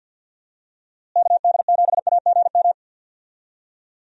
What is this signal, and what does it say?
Morse code, heard as a keyed tone, about 51 wpm, 700 Hz. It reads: KD7AGM
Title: USA Call Signs - 6 Characters - Repeated 2x - 50wpm